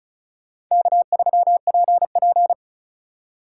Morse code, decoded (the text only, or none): K3PP